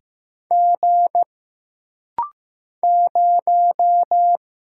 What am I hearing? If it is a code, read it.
Morse code, heard as a keyed tone, about 15 wpm, 700 Hz. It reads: G 0
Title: Single Letter-Number Alt - Rapid-Fire 15wpm